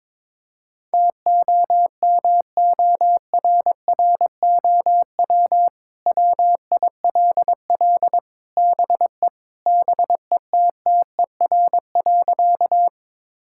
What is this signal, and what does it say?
Morse code, heard as a keyed tone, about 22 wpm, 700 Hz. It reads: TOMORROW WILL BE BETTER.
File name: practice / 22wpm / Sentences with 2-Word Phrases - 22wpm